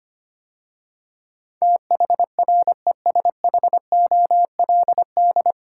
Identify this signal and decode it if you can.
Morse code, heard as a keyed tone, about 25 wpm, 700 Hz. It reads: THRESHOLD
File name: practice / 25wpm / 9 Letter Words - First 3 Letters Given - 25wpm